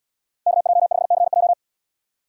Morse code, heard as a keyed tone, about 73 wpm, 700 Hz. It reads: KJ5CJ